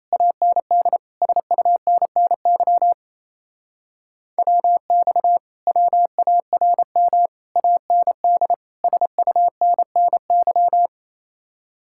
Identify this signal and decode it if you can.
Morse code, heard as a keyed tone, about 37 wpm, 700 Hz. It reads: AND SUNNY WX WARM AND SUNNY